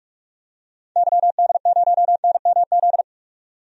Morse code, heard as a keyed tone, about 46 wpm, 700 Hz. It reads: YD0NMZ
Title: DXCC-100 Call Signs - 6 Characters - 45wpm